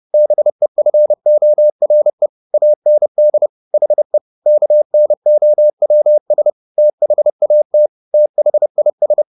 Morse code, heard as a keyed tone, about 30 wpm, 600 Hz. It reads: BEFORE AND HE KNOWS THAT THIS
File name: the old man and the sea / the old man and the sea 1 30wpm